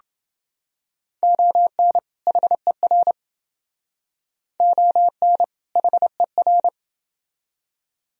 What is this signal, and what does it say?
Morse code, heard as a keyed tone, about 29 wpm, 700 Hz. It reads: ON HER ON HER